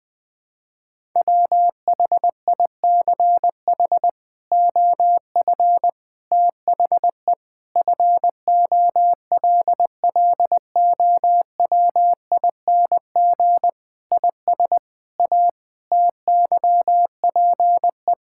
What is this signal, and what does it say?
Morse code, heard as a keyed tone, about 20 wpm, 700 Hz. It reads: WHICH OF THE FOLLOWING IS A TYPE
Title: Sentences with 8-Word Phrases - 20wpm